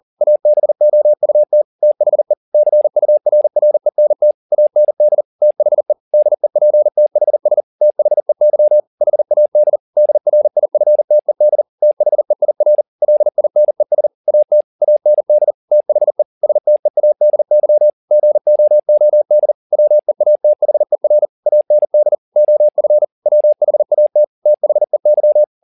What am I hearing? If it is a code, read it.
Morse code, heard as a keyed tone, about 40 wpm, 600 Hz. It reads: ABOUT THE CURRENT AND THE DEPTHS THEY HAD DRIFTED THEIR LINES AT AND THE STEADY GOOD WEATHER AND OF WHAT THEY